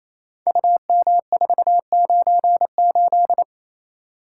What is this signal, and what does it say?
Morse code, heard as a keyed tone, about 27 wpm, 700 Hz. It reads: UM498
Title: Code Groups - 5 Letter:Numbers - 28wpm